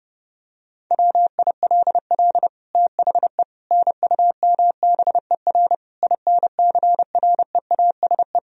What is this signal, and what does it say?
Morse code, heard as a keyed tone, about 30 wpm, 700 Hz. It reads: WILL THE NUMBER INCREASE
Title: Sentences from Top 1000 Words - Review Part 4 - 30wpm